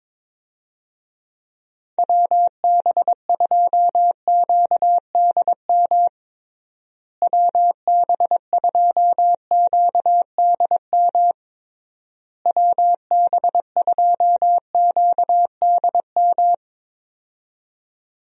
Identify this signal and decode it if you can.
Morse code, heard as a keyed tone, about 22 wpm, 700 Hz. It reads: WB2QDM WB2QDM WB2QDM
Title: USA Call Signs - 6 Characters - Repeated 3x - 22wpm